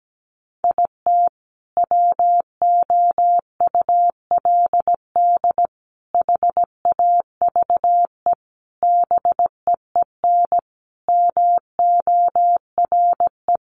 Morse code, heard as a keyed tone, about 17 wpm, 700 Hz. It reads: IT WOULD HAVE BEEN MORE
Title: Sentences from Top 400 Words 17wpm